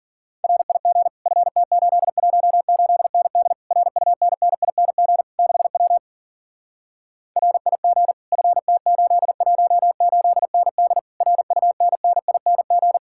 Morse code, heard as a keyed tone, about 48 wpm, 700 Hz. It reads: RIG FT818ND RUNNING 6W RIG FT818ND RUNNING
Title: Speed-Racing Common QSO Phrases - 40wpm